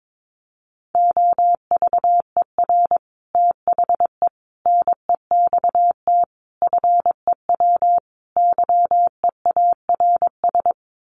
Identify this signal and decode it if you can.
Morse code, heard as a keyed tone, about 22 wpm, 700 Hz. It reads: OVER THE NEXT FEW YEARS
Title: Sets of 5 Words Encore 22wpm